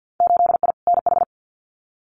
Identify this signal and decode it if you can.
Morse code, heard as a keyed tone, about 50 wpm, 700 Hz. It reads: 8I NH
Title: Winter Field Day Exchanges Encore 50wpm